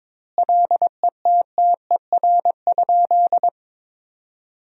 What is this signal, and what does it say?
Morse code, heard as a keyed tone, about 22 wpm, 700 Hz. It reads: LETTER?
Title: 2x Word-Spacing - Sentences from Top 300 Words - 22wpm